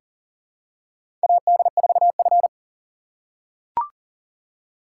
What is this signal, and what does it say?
Morse code, heard as a keyed tone, about 38 wpm, 700 Hz. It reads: AD4F
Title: USA Call Signs - 3x Character Spacing - 40wpm